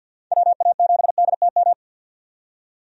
Morse code, heard as a keyed tone, about 53 wpm, 700 Hz. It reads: WA7DTM
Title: USA Call Signs - 6 Characters - 50wpm